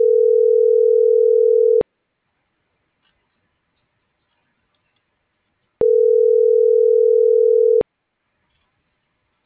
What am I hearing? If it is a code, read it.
Ringback tone (North America), 440 + 480 Hz.
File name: CENTRAL SOTOVA / out alejandro.rodriguez - .54401